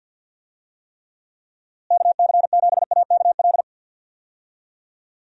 Morse code, heard as a keyed tone, about 51 wpm, 700 Hz. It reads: KC7AKL
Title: USA Call Signs - 6 Characters - Repeated 2x - 50wpm